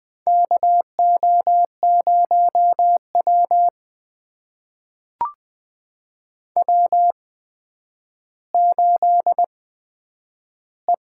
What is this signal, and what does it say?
Morse code, heard as a keyed tone, about 20 wpm, spaced as for 10 wpm, 700 Hz. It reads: KO0W W8E
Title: USA Call Signs - 4 Characters - 8x Character Spacing - 20wpm